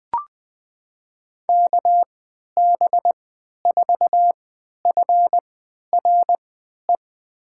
Morse code, heard as a keyed tone, about 20 wpm, spaced as for 11 wpm, 700 Hz. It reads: KB4FRE
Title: USA Call Signs - 6 Characters - 3x Character Spacing - 20wpm